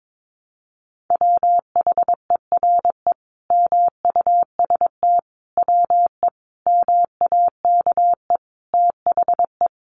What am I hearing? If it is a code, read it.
Morse code, heard as a keyed tone, about 22 wpm, 700 Hz. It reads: WHERE MUST WE MAKE THE
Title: Sentences from Top 1300 Words 22wpm